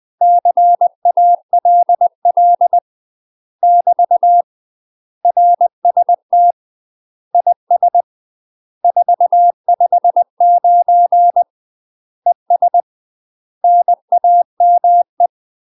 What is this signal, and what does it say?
Morse code, heard as a keyed tone, about 20 wpm, 700 Hz. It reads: CALL <BT> RST IS 459 ES NAME
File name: UK MASTER QSOs TRAINING/20WPM